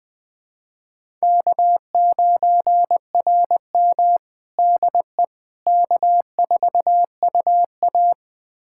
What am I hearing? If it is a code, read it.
Morse code, heard as a keyed tone, about 20 wpm, 700 Hz. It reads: K9RM DE K4UA
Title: USA Callsign DE Callsign - 4 Characters - 20wpm